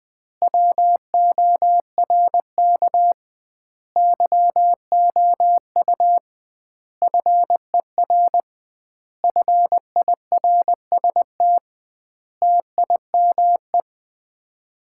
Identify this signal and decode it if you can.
Morse code, heard as a keyed tone, about 21 wpm, 700 Hz. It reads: WORK YOU FER FIRST TIME
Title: Common QSO Phrases - 2x Word-Spacing - 20wpm